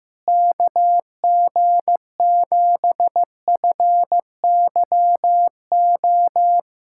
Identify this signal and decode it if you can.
Morse code, heard as a keyed tone, about 15 wpm, 700 Hz. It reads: KG7FYO
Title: USA Callsign DE Callsign 15wpm